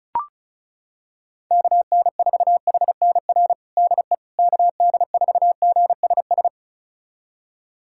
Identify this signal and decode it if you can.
Morse code, heard as a keyed tone, about 34 wpm, 700 Hz. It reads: KN4HNR DE KD4GSS
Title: USA Callsign DE Callsign - 6 Characters - 35wpm